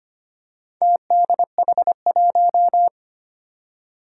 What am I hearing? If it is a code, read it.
Morse code, heard as a keyed tone, about 25 wpm, 700 Hz. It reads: TDH1
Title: Four Challenging Letter-Number Combinations 25wpm